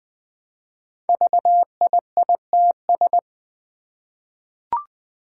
Morse code, heard as a keyed tone, about 20 wpm, 700 Hz. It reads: VIITS
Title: Five Challenging Letter-Number Combinations 20wpm